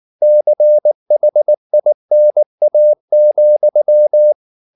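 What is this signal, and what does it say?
Morse code, heard as a keyed tone, about 19 wpm, 600 Hz. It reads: CHINA,
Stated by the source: Morse code at 19 wpm from CW med Gnister, reading Ken Follett's novel Never in English.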